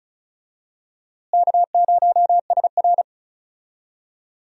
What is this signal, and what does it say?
Morse code, heard as a keyed tone, about 36 wpm, 700 Hz. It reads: K0SR
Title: USA Call Signs - 4 Characters - Repeated 2x - 35wpm